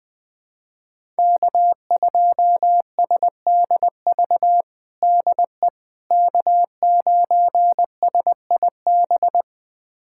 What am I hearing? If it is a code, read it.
Morse code, heard as a keyed tone, about 21 wpm, 700 Hz. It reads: K2SDV DE K9SIB
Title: USA Callsign DE Callsign - 5 Characters - 20wpm